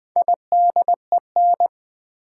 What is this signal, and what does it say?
Morse code, heard as a keyed tone, about 20 wpm, 700 Hz. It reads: IDEN
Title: Build A Word - 2 Letters At A Time - 8-Letter Words - 20wpm